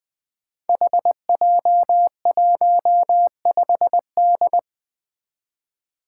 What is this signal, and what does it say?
Morse code, heard as a keyed tone, about 20 wpm, 700 Hz. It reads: HJ15D
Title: Five Challenging Letter-Number Combinations 20wpm